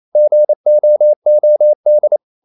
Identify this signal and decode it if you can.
Morse code, heard as a keyed tone, about 28 wpm, 600 Hz. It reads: GOOD
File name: CW / Practice / EnglishSayings28